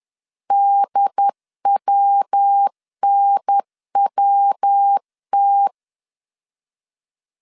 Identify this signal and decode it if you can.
Morse code, heard as a keyed tone, about 10 wpm, 800 Hz. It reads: DWNWT
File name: CW / DET / 1 3 1500C S C 10wpm 3335